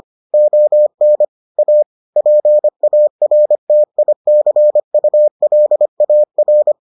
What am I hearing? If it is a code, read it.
Morse code, heard as a keyed tone, about 25 wpm, 600 Hz. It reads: ON A PARTICULAR